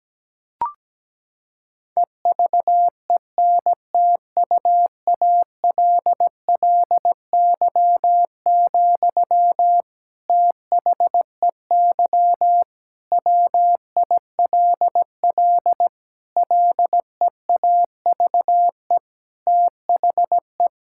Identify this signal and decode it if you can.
Morse code, heard as a keyed tone, about 17 wpm, 700 Hz. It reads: EVENTUALLY, THEY WILL LEAVE THE
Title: Sentences from Top 1000 Words - Review Part 6 - 17wpm